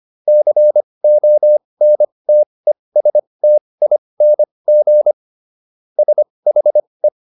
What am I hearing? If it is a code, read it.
Morse code, heard as a keyed tone, about 25 wpm, spaced as for 20 wpm, 600 Hz. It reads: CONTESTING SHE